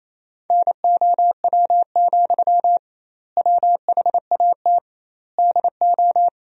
Morse code, heard as a keyed tone, about 28 wpm, 700 Hz. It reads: NOW, WHAT DO
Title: 2x Word-Spacing - Sentences from Top 200 Words - 28wpm